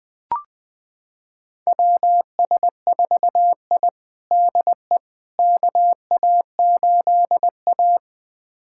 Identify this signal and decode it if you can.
Morse code, heard as a keyed tone, about 20 wpm, 700 Hz. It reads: WS4I DE KA8A
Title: USA Callsign DE Callsign - 4 Characters - 20wpm